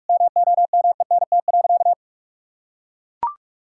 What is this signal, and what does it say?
Morse code, heard as a keyed tone, about 41 wpm, 700 Hz. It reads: MOMENT.
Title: Sentences from Top 1000 Words - Review Part 6 - 45wpm